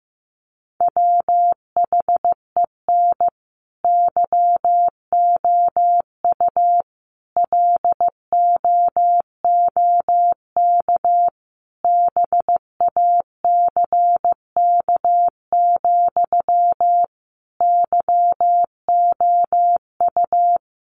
Morse code, heard as a keyed tone, about 15 wpm, 700 Hz. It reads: WHEN YOU LOOK BACK, YOU
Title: Sentences from Top 800 Words 15wpm